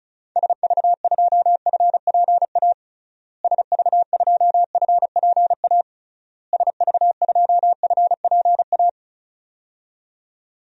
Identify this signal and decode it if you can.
Morse code, heard as a keyed tone, about 34 wpm, 700 Hz. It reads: SV2FPA SV2FPA SV2FPA
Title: DXCC-100 Call Signs - 6 Characters - Repeated 3x - 35wpm